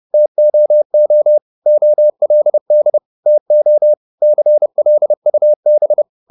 Morse code, heard as a keyed tone, about 30 wpm, 600 Hz. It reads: TOO OLD TO CLUB